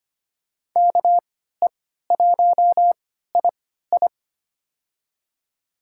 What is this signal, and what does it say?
Morse code, heard as a keyed tone, about 25 wpm, spaced as for 14 wpm, 700 Hz. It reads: KE1II
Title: USA Call Signs - 5 Characters - 3x Character Spacing - 25wpm